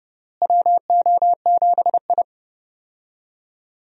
Morse code, heard as a keyed tone, about 30 wpm, 700 Hz. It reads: WO7I
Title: USA Call Signs - 4 Characters - Repeated 3x - 30wpm